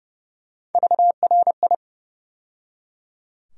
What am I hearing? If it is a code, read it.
Morse code, heard as a keyed tone, about 30 wpm, 700 Hz. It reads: VRI
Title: USA Call Sign Postfixes - 3 Letters 30wpm